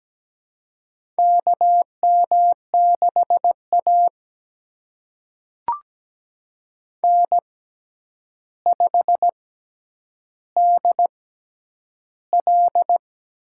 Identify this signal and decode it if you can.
Morse code, heard as a keyed tone, about 17 wpm, spaced as for 9 wpm, 700 Hz. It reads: KM6A N5DL
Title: USA Call Signs - 4 Characters - 6x Character Spacing - 17wpm